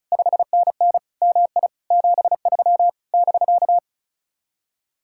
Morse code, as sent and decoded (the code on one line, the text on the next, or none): ..... -. -. / -- .. / --... ...-- / -...-.-
5NN MI 73 <BK>